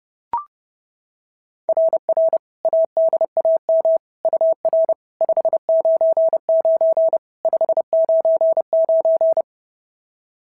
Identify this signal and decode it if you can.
Morse code, heard as a keyed tone, about 30 wpm, 690 Hz. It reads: RR ADAM UR 599 599
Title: SOTA QSO Practice 30wpm